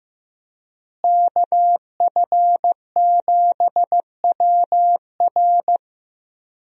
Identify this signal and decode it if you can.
Morse code, heard as a keyed tone, about 15 wpm, 700 Hz. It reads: KF7WR
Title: USA Call Signs - 5 Characters - Repeated 3x - 15wpm